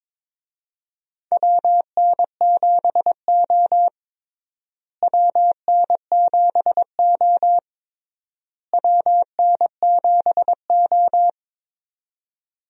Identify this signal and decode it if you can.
Morse code, heard as a keyed tone, about 22 wpm, 700 Hz. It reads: WN7O WN7O WN7O
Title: USA Call Signs - 4 Characters - Repeated 3x - 22wpm